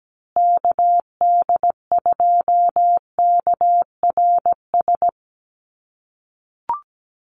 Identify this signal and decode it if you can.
Morse code, heard as a keyed tone, about 17 wpm, 700 Hz. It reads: KD2KRS
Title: USA Call Signs Repeated 2x 17wpm